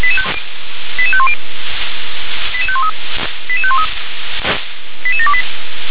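Five-tone selective call (ZVEI), heard as quick runs of five tones, tones of about 70 ms. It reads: 88510 88513 88513 88728
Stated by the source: Received on SDR.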